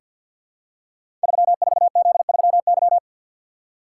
Morse code, heard as a keyed tone, about 49 wpm, 700 Hz. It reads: 34Z3Y
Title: Code Groups - 5 Letter:Numbers - 50wpm